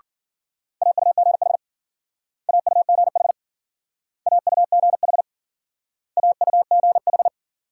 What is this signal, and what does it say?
Morse code, heard as a keyed tone, about 53 wpm, 700 Hz. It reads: AUGH AUGH AUGH AUGH